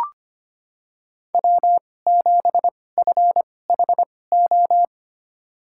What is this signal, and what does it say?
Morse code, heard as a keyed tone, about 25 wpm, spaced as for 18 wpm, 700 Hz. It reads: W7FHO
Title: USA Call Signs - 5 Characters - 2x Character Spacing - 25wpm